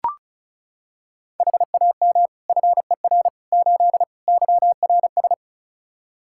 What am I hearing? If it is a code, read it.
Morse code, heard as a keyed tone, about 34 wpm, 700 Hz. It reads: HAM FER 8 YRS
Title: Practice QSO - Exchange 3 - More Info - 35wpm